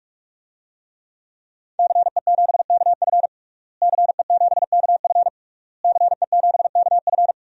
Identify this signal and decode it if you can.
Morse code, heard as a keyed tone, about 46 wpm, 700 Hz. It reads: CE7KF CE7KF CE7KF
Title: DXCC-100 Call Signs - 5 Characters - Repeated 3x - 45wpm